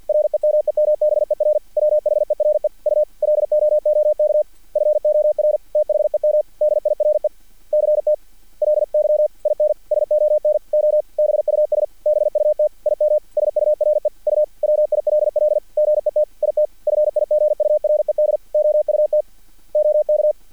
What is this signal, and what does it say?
Morse code, heard as a keyed tone, about 50 wpm, 600 Hz. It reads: REMEMBER WHERE U LOOK FOR THEM DIRECTLY IN SONOBUS BUT IM SURE U WILL GET IT FIGURED OUT OK
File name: archived / CRSnet